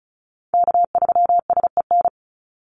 Morse code, heard as a keyed tone, about 34 wpm, 700 Hz. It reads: K3SEN
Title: Call Signs Repeated 2x - 35wpm